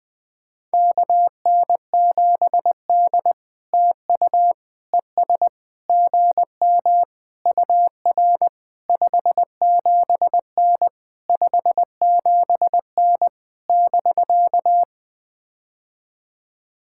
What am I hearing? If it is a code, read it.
Morse code, heard as a keyed tone, about 21 wpm, 700 Hz. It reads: KN7D TU ES GM UR 57N 57N <BK>
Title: POTA QSO Practice - 20wpm